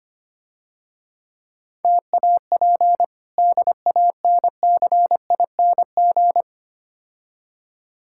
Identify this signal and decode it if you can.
Morse code, heard as a keyed tone, about 25 wpm, 690 Hz. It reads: TAP DANCING
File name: practice / 25wpm / Difficult Words - Speed-Racing with Hints - 25wpm